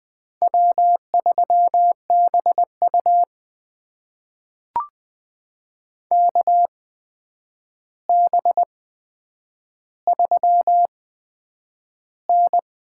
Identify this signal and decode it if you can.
Morse code, heard as a keyed tone, about 20 wpm, spaced as for 9 wpm, 700 Hz. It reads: W3BU KB3N
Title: USA Call Signs - 4 Characters - 8x Character Spacing - 20wpm